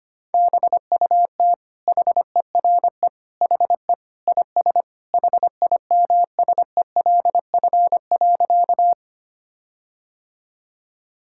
Morse code, heard as a keyed tone, about 25 wpm, 700 Hz. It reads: BUT HERE HE IS HIMSELF.
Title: Sentences from Top 1600 Words 25wpm